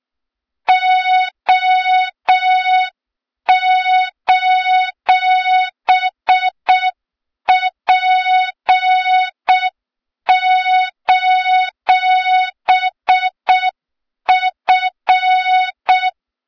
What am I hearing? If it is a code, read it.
Morse code, heard as a keyed tone, about 6 wpm, 3700 Hz. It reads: O:P:F
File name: Morze-HA2MN / hang / 23 GYAKORLAT